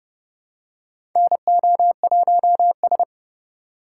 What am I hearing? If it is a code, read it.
Morse code, heard as a keyed tone, about 30 wpm, 700 Hz. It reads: NO1S
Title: USA Call Signs - 4 Characters - 6x Character Spacing - 30wpm